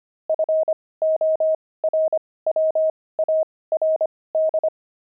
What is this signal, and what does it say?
Morse code, heard as a keyed tone, about 25 wpm, spaced as for 18 wpm, 630 Hz. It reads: FORWARD
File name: pangram-fs-18-cs-25